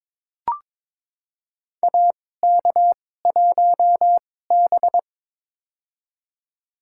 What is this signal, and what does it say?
Morse code, heard as a keyed tone, about 22 wpm, spaced as for 16 wpm, 700 Hz. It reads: AK1B
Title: USA Call Signs - 4 Characters - 2x Character Spacing - 22wpm